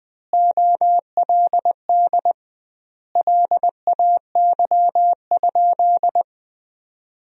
Morse code, heard as a keyed tone, about 20 wpm, 700 Hz. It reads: OLD LAY?
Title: 2x Word-Spacing - Sentences from Top 400 Words - 20wpm